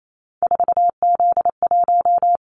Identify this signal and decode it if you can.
Morse code, heard as a keyed tone, about 28 wpm, 700 Hz. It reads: 4Z1
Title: Callsign Prefixes for Top 50 DXCC - 28wpm